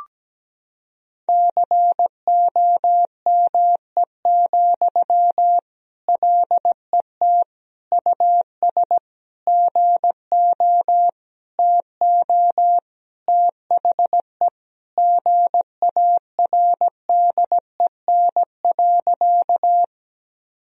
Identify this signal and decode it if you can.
Morse code, heard as a keyed tone, about 17 wpm, 700 Hz. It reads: COME, LET US GO TO THE GARDEN.
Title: Sentences from Top 1000 Words - Review Part 4 - 17wpm